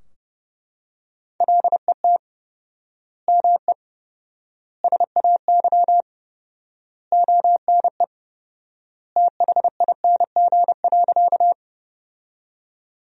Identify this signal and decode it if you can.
Morse code, heard as a keyed tone, about 30 wpm, 700 Hz. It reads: LET ME SAY ONE THING.
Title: 4x Word-Spacing - Sentences from Top 300 Words - 30wpm